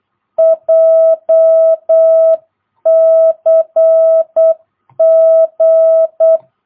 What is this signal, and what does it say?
Morse code, heard as a keyed tone, about 8 wpm, 660 Hz. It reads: JCG
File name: jh8cbh / 2022cw